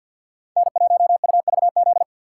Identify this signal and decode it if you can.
Morse code, heard as a keyed tone, about 51 wpm, 700 Hz. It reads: N1UVZ
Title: USA Call Signs - 5 Characters - 2x Character Spacing - 50wpm